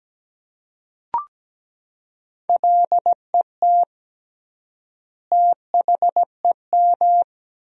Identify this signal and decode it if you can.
Morse code, heard as a keyed tone, about 17 wpm, 700 Hz. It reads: LET THEM
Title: 3x Word-Spacing - Sentences from Top 300 Words - 17wpm